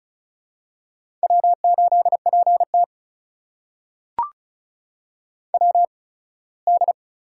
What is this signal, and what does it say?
Morse code, heard as a keyed tone, about 36 wpm, spaced as for 21 wpm, 700 Hz. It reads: W8PT WD
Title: USA Call Signs - 4 Characters - 8x Character Spacing - 35wpm